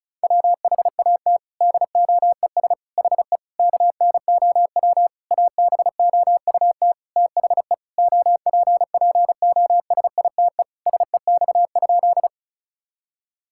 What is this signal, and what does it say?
Morse code, heard as a keyed tone, about 36 wpm, 700 Hz. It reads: WHAT DOES HE KNOW ABOUT THE OPPOSITE SEX?
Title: Sentences from Top 1600 Words 35wpm